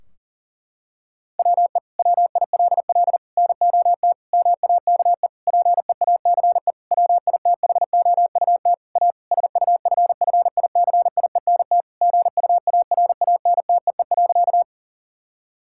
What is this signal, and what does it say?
Morse code, heard as a keyed tone, about 40 wpm, 700 Hz. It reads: WE WILL NOT MAKE PEACE WITHOUT A SUFFICIENT GUARANTEE.
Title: Sentences from Top 1600 Words 40wpm